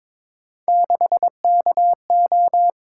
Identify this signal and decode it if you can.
Morse code, heard as a keyed tone, about 22 wpm, 700 Hz. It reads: 6KO
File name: practice / 22wpm / Three Letter-Number Combinations - 22wpm